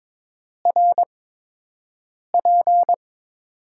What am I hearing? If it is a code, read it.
Morse code, heard as a keyed tone, about 22 wpm, spaced as for 6 wpm, 700 Hz. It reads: RP
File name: Top 90 QSO Elements - 8x Character Spacing - 22wpm